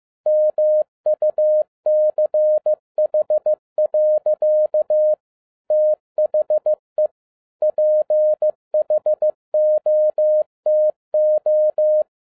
Morse code, heard as a keyed tone, about 15 wpm, 610 Hz. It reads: MUCH. THE PHOTO
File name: 2012-01-mp3 / n07-ARA JR0DIL-15wpm